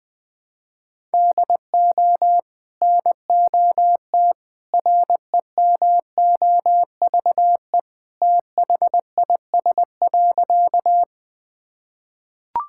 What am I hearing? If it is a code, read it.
Morse code, heard as a keyed tone, about 20 wpm, 700 Hz. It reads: DO NOT REMOVE THIS.
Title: Sentences from Top 1000 Words - Review Part 5 - 20wpm